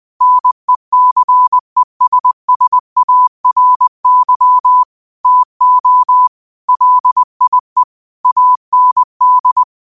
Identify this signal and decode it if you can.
Morse code, heard as a keyed tone, about 20 wpm, 1000 Hz. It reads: NECESSARY TO LIE AND
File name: 20wpm